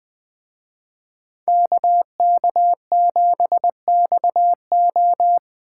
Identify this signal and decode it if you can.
Morse code, heard as a keyed tone, about 20 wpm, 700 Hz. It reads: KK7XO